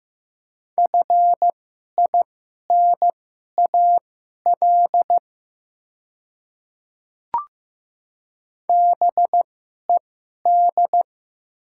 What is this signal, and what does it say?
Morse code, heard as a keyed tone, about 15 wpm, spaced as for 11 wpm, 700 Hz. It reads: FINAL BED
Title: 2x Character Spacing - Top 500 Words - 15wpm